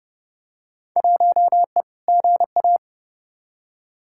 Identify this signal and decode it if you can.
Morse code, heard as a keyed tone, about 30 wpm, 700 Hz. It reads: 1E GA
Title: Field Day Exchanges - 30wpm